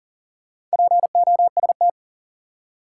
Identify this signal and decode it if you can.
Morse code, heard as a keyed tone, about 40 wpm, 700 Hz. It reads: POST
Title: Four Letter Words - 40wpm